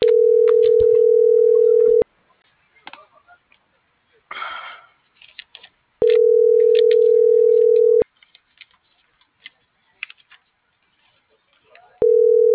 Ringback tone (North America), 440 + 480 Hz.